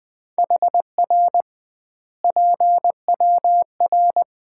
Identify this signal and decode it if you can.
Morse code, heard as a keyed tone, about 20 wpm, 700 Hz. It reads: HR PWR
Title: Common QSO Phrases - 2x Word-Spacing - 20wpm